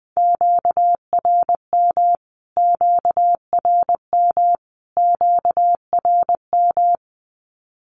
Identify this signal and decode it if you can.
Morse code, heard as a keyed tone, about 20 wpm, 700 Hz. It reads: QRM QRM QRM
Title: Q-Codes Repeated 3x 20wpm